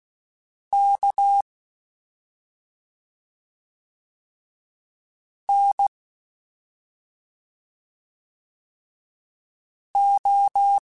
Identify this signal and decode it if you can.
Morse code, heard as a keyed tone, about 16 wpm, spaced as for 2 wpm, 800 Hz. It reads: KNO
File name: Special INGs-mp3s / Special INGs-mp3s / Slow ing @10